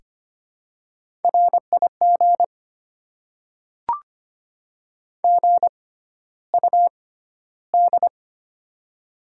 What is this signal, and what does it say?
Morse code, heard as a keyed tone, about 25 wpm, spaced as for 13 wpm, 700 Hz. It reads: RIG GUD
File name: Top 20 QSO Elements - 6x Character Spacing - 25wpm